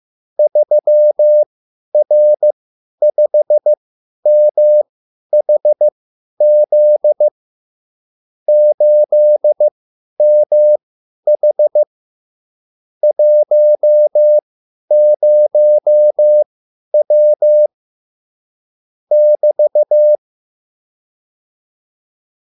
Morse code, heard as a keyed tone, about 15 wpm, spaced as for 10 wpm, 600 Hz. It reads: 3R5MHZ 8MH 10W <BT>
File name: QSL / CW lesson